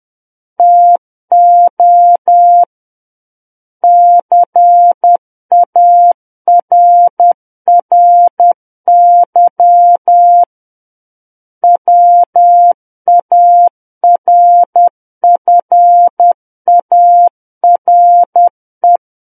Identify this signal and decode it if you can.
Morse code, heard as a keyed tone, about 10 wpm, 700 Hz. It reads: TO CARRY WARFARE
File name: Book One / 01-Chapter 1 - 10 WPM